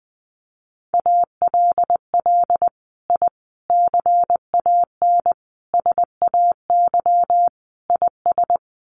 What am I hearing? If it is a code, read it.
Morse code, heard as a keyed tone, about 20 wpm, 700 Hz. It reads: ALL I CAN SAY IS
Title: Sets of 5 Words Encore2 20wpm